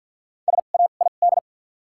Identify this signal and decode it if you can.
Morse code, heard as a keyed tone, about 53 wpm, spaced as for 37 wpm, 700 Hz. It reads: SAID
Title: 2x Character Spacing - Top 200 Words - 50wpm